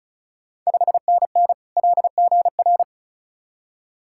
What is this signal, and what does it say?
Morse code, heard as a keyed tone, about 36 wpm, 700 Hz. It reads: 5NN LGR
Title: Ontario QSO Party Exchange 35wpm